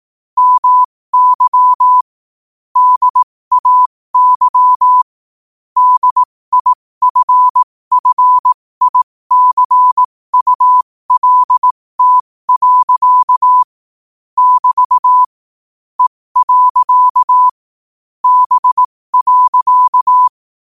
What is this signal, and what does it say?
Morse code, heard as a keyed tone, about 18 wpm, spaced as for 15 wpm, 1000 Hz. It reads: MY DAY DIFFICULT. <BT> E. B.
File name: qotdfiles / 15wpm